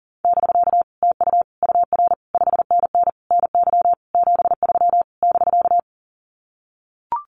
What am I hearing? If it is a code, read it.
Morse code, heard as a keyed tone, about 40 wpm, 700 Hz. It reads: <BK> TU UR 5NN NY 73 <BK>
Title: Common QSO Phrases Encore 40wpm